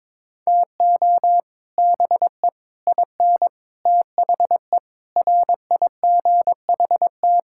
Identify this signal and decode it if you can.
Morse code, heard as a keyed tone, about 22 wpm, 700 Hz. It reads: TO BE IN THE RIGHT
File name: Sets of 9 Words - 22wpm